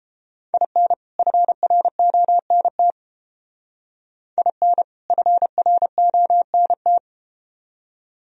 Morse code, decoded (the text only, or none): IN FRONT IN FRONT